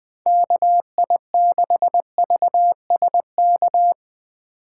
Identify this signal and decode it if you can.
Morse code, heard as a keyed tone, about 21 wpm, 700 Hz. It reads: KI6VSK